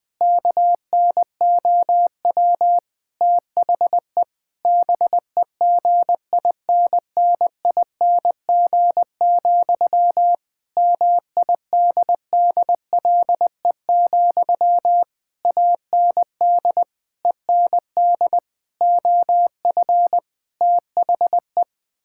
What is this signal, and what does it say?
Morse code, heard as a keyed tone, about 20 wpm, 700 Hz. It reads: KNOW THE BEGINNING, MIDDLE, AND END OF THE
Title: Sentences from Top 1000 Words - Review Part 6 - 20wpm